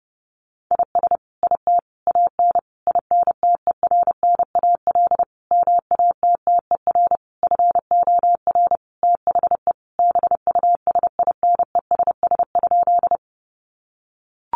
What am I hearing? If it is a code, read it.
Morse code, heard as a keyed tone, about 30 wpm, 700 Hz. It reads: IS IT AN INTERNAL MATTER FOR THE BUSINESS?
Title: Sentences from Top 1400 Words 30wpm